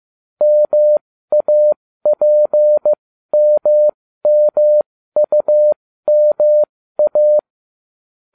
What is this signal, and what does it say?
Morse code, heard as a keyed tone, about 15 wpm, spaced as for 13 wpm, 600 Hz. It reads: MAPMMUMA